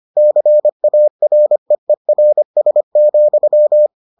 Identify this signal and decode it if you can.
Morse code, heard as a keyed tone, about 25 wpm, 600 Hz. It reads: CAREERS,